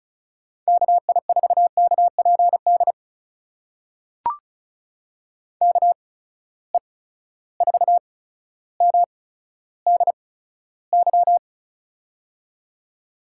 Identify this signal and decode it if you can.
Morse code, heard as a keyed tone, about 36 wpm, spaced as for 15 wpm, 700 Hz. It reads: KI4KPD KE4MDY
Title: USA Call Signs - 8x Character Spacing - 35wpm